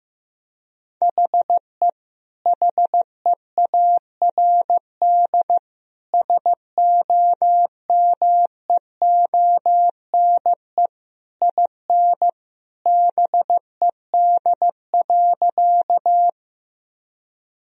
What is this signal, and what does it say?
Morse code, heard as a keyed tone, about 15 wpm, 700 Hz. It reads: HE HEARD SOMEONE IN BED.
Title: Sentences from Top 1000 Words - Review Part 4 - 15wpm